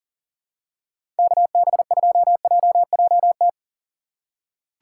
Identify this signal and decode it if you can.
Morse code, heard as a keyed tone, about 40 wpm, 700 Hz. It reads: KB2JJT
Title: USA Call Signs - 6 Characters - 4x Character Spacing - 40wpm